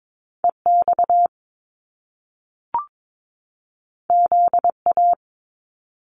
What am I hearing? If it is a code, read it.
Morse code, heard as a keyed tone, about 22 wpm, 700 Hz. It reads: EX ZA
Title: Top Two Letter Combinations in Words 22wpm